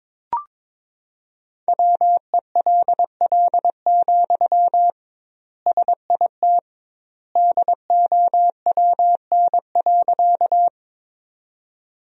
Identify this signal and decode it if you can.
Morse code, heard as a keyed tone, about 21 wpm, 700 Hz. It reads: WELL, SIT DOWN.
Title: 2x Word-Spacing - Sentences from Top 500 Words - 22wpm